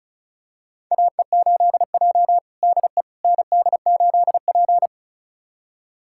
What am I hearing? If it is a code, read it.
Morse code, heard as a keyed tone, about 36 wpm, 700 Hz. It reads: AE8J DE ND8P
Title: USA Callsign DE Callsign - 4 Characters - 35wpm